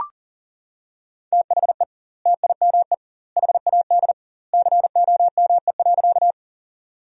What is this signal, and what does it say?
Morse code, heard as a keyed tone, about 37 wpm, 700 Hz. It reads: THE TIME HAD COME.
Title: 2x Word-Spacing - Sentences from Top 100 Words - 40wpm